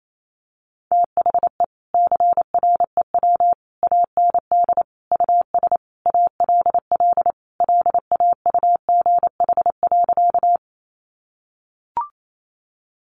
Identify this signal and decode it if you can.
Morse code, heard as a keyed tone, about 28 wpm, 700 Hz. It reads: THE CREW AND US ALL LAUGH.
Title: Sentences from Top 1500 Words 28wpm